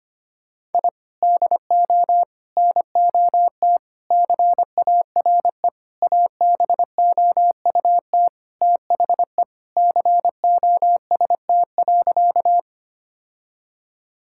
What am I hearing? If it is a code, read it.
Morse code, heard as a keyed tone, about 25 wpm, 700 Hz. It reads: I DO NOT CARE ABOUT THE COST.
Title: Sentences from Top 1000 Words - Review Part 5 - 25wpm